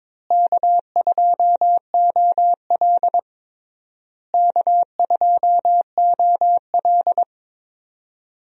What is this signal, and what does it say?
Morse code, heard as a keyed tone, about 22 wpm, 700 Hz. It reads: K2OL K2OL
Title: USA Call Signs - 4 Characters - Repeated 3x - 22wpm